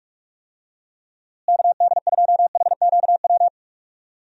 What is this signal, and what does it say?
Morse code, heard as a keyed tone, about 46 wpm, 700 Hz. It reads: KD2HQW
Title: USA Call Signs - 6 Characters - Repeated 2x - 45wpm